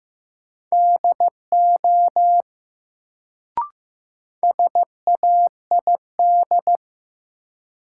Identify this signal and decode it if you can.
Morse code, heard as a keyed tone, about 16 wpm, 700 Hz. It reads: DO SAID
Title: Sentences from Top 100 Words - New Word Drill - 15wpm